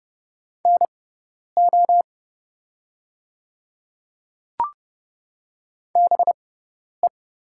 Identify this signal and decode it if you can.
Morse code, heard as a keyed tone, about 30 wpm, spaced as for 10 wpm, 700 Hz. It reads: NO BE
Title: 6x Character Spacing - Top 200 Words - 30wpm